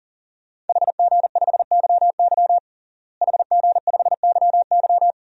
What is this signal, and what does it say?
Morse code, heard as a keyed tone, about 40 wpm, 700 Hz. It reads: HG5YY HG5YY
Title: DXCC-100 Call Signs - 5 Characters - Repeated 3x - 40wpm